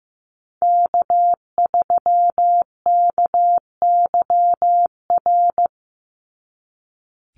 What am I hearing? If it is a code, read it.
Morse code, heard as a keyed tone, about 15 wpm, 700 Hz. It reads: K3KYR
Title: NA SOTA Chaser Callsigns 15wpm